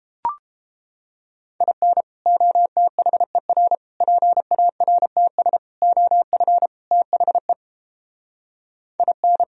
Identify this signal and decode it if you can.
Morse code, heard as a keyed tone, about 31 wpm, 700 Hz. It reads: IN OTHER PARTS OF THE IN